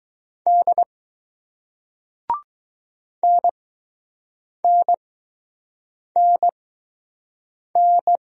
Morse code, heard as a keyed tone, about 20 wpm, spaced as for 6 wpm, 700 Hz. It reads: D NNNN